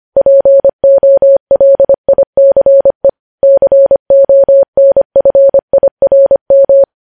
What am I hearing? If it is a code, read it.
Morse code, heard as a keyed tone, about 25 wpm, 550 Hz. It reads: POLICE CONFIRM
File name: podcast / News headlines-1x-25x25